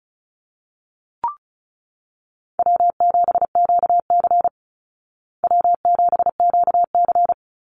Morse code, heard as a keyed tone, about 34 wpm, 700 Hz. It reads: W7QC W7QC
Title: Call Signs Repeated 3x - 35wpm